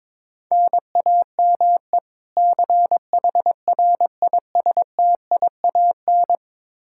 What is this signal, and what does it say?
Morse code, heard as a keyed tone, about 22 wpm, 700 Hz. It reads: NAME CHRISTIAN
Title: Practice QSO - Exchange 1 - Beginning - 22wpm